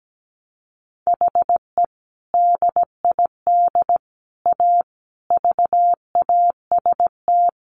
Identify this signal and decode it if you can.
Morse code, heard as a keyed tone, about 17 wpm, 700 Hz. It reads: HE DID A VAST
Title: Sentences from Top 1300 Words 17wpm